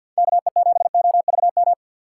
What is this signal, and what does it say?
Morse code, heard as a keyed tone, about 52 wpm, 700 Hz. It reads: KE7OVM
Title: USA Call Signs - 6 Characters - 4x Character Spacing - 50wpm